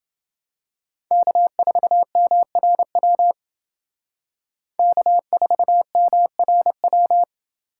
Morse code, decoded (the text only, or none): K4MRW K4MRW